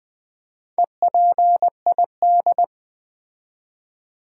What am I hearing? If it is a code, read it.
Morse code, heard as a keyed tone, about 20 wpm, 700 Hz. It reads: EPID
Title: Build A Word - 2 Letters At A Time - 8-Letter Words - 20wpm